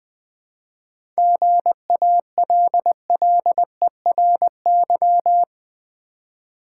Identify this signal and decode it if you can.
Morse code, heard as a keyed tone, about 20 wpm, 700 Hz. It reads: GALLERY
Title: 7 Letter Words - First 3 Letters Given - 20wpm